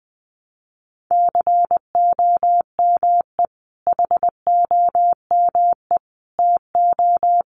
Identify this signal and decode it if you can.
Morse code, heard as a keyed tone, about 20 wpm, 700 Hz. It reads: COME HOME TO
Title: Sentences from Top 1000 Words 20wpm